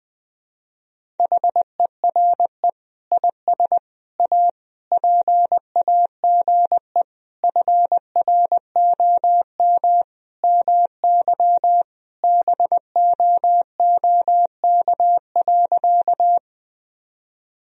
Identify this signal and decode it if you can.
Morse code, heard as a keyed tone, about 20 wpm, 700 Hz. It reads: HERE IS A PAGE FROM MY BOOK.
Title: Sentences from Top 1000 Words - Review Part 1 - 20wpm